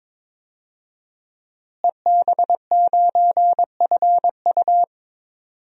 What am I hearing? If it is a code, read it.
Morse code, heard as a keyed tone, about 22 wpm, 700 Hz. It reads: EB9FU
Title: DXCC-100 Call Signs - 5 Characters - Repeated 3x - 22wpm